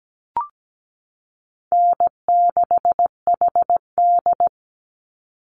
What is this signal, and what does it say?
Morse code, heard as a keyed tone, about 17 wpm, 700 Hz. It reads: N6HD
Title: Easy Call Signs Repeated 2x - 17wpm